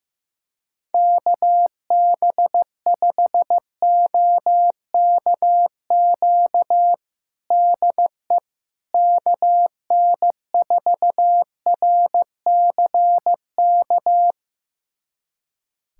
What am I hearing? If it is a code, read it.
Morse code, heard as a keyed tone, about 15 wpm, 700 Hz. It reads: KB5OKQ DE KN4RCK
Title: USA Callsign DE Callsign 15wpm